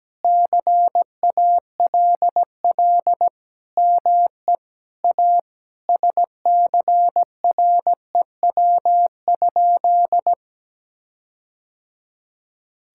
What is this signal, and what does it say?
Morse code, heard as a keyed tone, about 17 wpm, 700 Hz. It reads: CALL ME A SCREW?
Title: Sentences from Top 1600 Words 17wpm